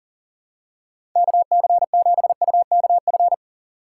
Morse code, heard as a keyed tone, about 40 wpm, 700 Hz. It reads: KC7UKF
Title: USA Call Signs - 6 Characters - 3x Character Spacing - 40wpm